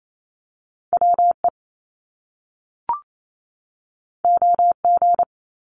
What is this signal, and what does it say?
Morse code, heard as a keyed tone, about 28 wpm, 700 Hz. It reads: WE OG
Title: Top Two Letter Combinations in Words 28wpm